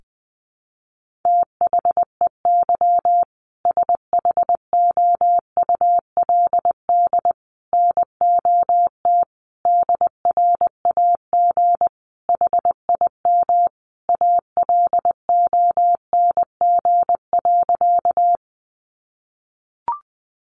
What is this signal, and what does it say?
Morse code, heard as a keyed tone, about 20 wpm, 700 Hz. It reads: THEY SHOULD NOT DRAG HIM ALONG.
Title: Sentences from Top 1500 Words 20wpm